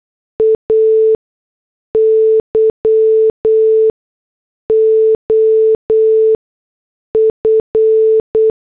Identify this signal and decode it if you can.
Morse code, heard as a keyed tone, about 8 wpm, spaced as for 6 wpm, 440 Hz. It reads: AY OF